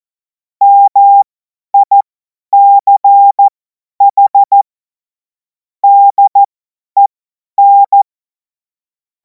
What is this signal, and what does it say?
Morse code, heard as a keyed tone, about 14 wpm, spaced as for 10 wpm, 800 Hz. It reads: MICH DEN
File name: Morsekurs / Witze / Witz 17. März 2026 50 F70